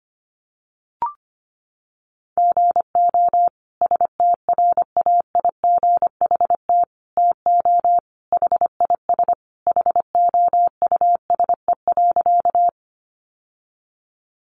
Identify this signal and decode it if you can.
Morse code, heard as a keyed tone, about 25 wpm, 700 Hz. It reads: GO STRAIGHT TO HIS HOUSE.
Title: Sentences from Top 1000 Words 25wpm